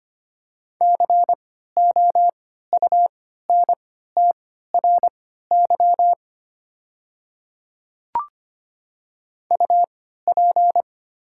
Morse code, heard as a keyed tone, about 25 wpm, spaced as for 14 wpm, 700 Hz. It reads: COUNTRY UP
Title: 3x Character Spacing - Top 500 Words - 25wpm